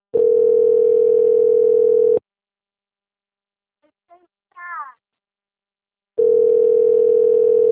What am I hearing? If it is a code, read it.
Ringback tone (North America), 440 + 480 Hz.
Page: Braganca - SP